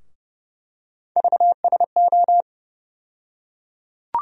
Morse code, heard as a keyed tone, about 30 wpm, 700 Hz. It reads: VSO
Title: USA Call Sign Postfixes - 3 Letters 30wpm